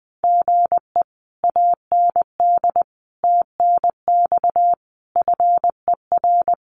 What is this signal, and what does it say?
Morse code, heard as a keyed tone, about 20 wpm, 700 Hz. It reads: GE AND TNX FER
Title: Common QSO Phrases Encore 20wpm